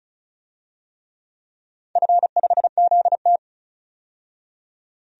Morse code, heard as a keyed tone, about 36 wpm, 700 Hz. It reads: F5ZT
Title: DXCC-100 Call Signs - 4 Characters - 35wpm